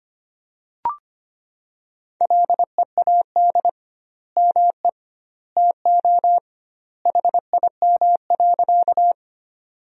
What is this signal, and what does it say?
Morse code, heard as a keyed tone, about 25 wpm, 700 Hz. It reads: LEAD ME TO HIM.
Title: 2x Word-Spacing - Sentences from Top 500 Words - 25wpm